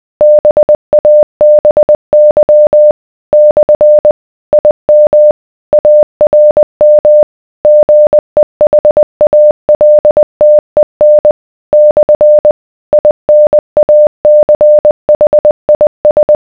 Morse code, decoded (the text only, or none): BABY / IM ARM GEHALTEN / INACHIS